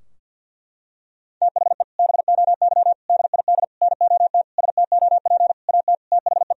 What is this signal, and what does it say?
Morse code, heard as a keyed tone, about 50 wpm, 700 Hz. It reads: THE BOY DID NOT STOP AT THE
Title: Sentences from Top 1000 Words - Review Part 2 - 50wpm